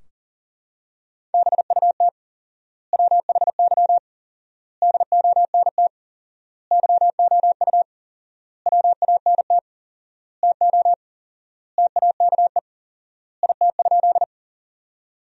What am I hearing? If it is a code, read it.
Morse code, heard as a keyed tone, about 40 wpm, 700 Hz. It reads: BUT WHY DONT YOU WANT TO TAKE IT?
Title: 4x Word-Spacing - Sentences from Top 300 Words - 40wpm